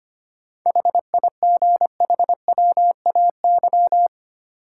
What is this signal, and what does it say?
Morse code, heard as a keyed tone, about 25 wpm, 700 Hz. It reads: HIGHWAY
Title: Sentences from Top 1400 Words - New Word Drill Alt - 25wpm